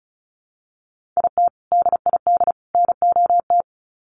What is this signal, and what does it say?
Morse code, heard as a keyed tone, about 34 wpm, 700 Hz. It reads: IT DID NOT
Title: Sets of 3 Words 35wpm